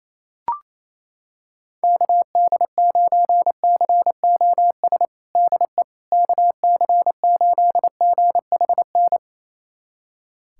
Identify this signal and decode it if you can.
Morse code, heard as a keyed tone, about 28 wpm, 700 Hz. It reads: KD9COS DE KC8GHN
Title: USA Callsign DE Callsign - 6 Characters - 28wpm